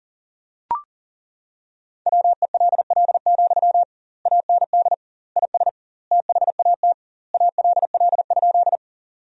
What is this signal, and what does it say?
Morse code, heard as a keyed tone, about 40 wpm, 700 Hz. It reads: WELL, AND IS THAT ALL?
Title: 2x Word-Spacing - Sentences from Top 200 Words - 40wpm